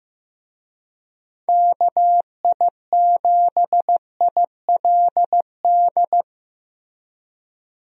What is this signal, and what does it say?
Morse code, heard as a keyed tone, about 15 wpm, 700 Hz. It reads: KI7ILD
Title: USA Call Signs - 6 Characters - Repeated 3x - 15wpm